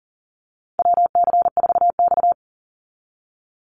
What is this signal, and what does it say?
Morse code, heard as a keyed tone, about 40 wpm, 700 Hz. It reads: WC4X
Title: Call Signs - 40wpm